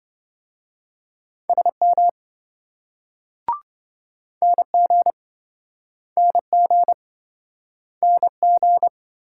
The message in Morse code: SM NG NG NG

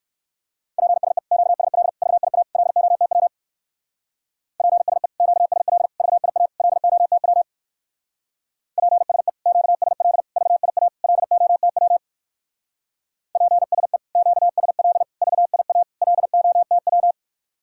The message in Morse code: PSE QSL VIA LOTW PSE QSL VIA LOTW PSE QSL VIA LOTW PSE QSL VIA LOTW

